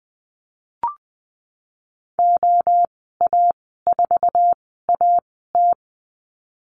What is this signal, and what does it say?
Morse code, heard as a keyed tone, about 20 wpm, spaced as for 15 wpm, 700 Hz. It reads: OA4AT
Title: 173 Lesson 28g - Number 9 - Call Signs 20wpm